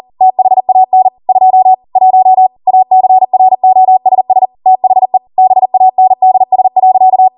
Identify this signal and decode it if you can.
Morse code, heard as a keyed tone, about 40 wpm, 750 Hz. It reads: THAN 2 1 ACROSS THE BANDS.